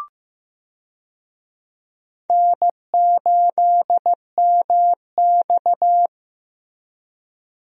Morse code, heard as a keyed tone, about 15 wpm, 700 Hz. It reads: N8MX
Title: USA Call Signs - 4 Characters - Repeated 2x - 15wpm